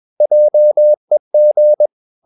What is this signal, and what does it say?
Morse code, heard as a keyed tone, about 21 wpm, 600 Hz. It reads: JEG
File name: Ferie QRS 21wpm